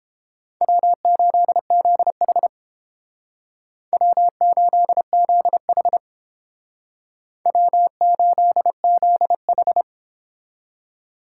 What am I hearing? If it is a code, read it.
Morse code, heard as a keyed tone, about 29 wpm, 700 Hz. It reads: W8ZH W8ZH W8ZH